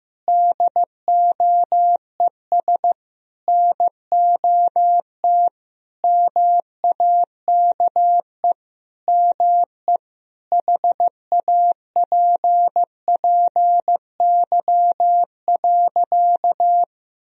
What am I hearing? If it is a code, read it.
Morse code, heard as a keyed tone, about 16 wpm, 700 Hz. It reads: DOES NOT MAKE ME HAPPY.
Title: Sentences from Top 1000 Words - Review Part 2 - 15wpm